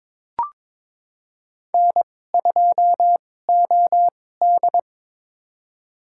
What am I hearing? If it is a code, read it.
Morse code, heard as a keyed tone, about 22 wpm, spaced as for 16 wpm, 700 Hz. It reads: N2OD
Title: USA Call Signs - 4 Characters - 2x Character Spacing - 22wpm